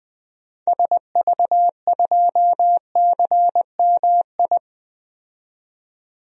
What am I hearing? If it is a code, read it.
Morse code, heard as a keyed tone, about 20 wpm, 700 Hz. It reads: SV2CMI
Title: DXCC-100 Call Signs - 6 Characters - 20wpm